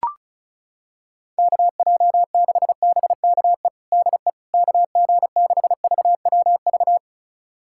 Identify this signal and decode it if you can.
Morse code, heard as a keyed tone, about 34 wpm, 700 Hz. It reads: KJ6BKE DE KG6VWV
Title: USA Callsign DE Callsign - 6 Characters - 35wpm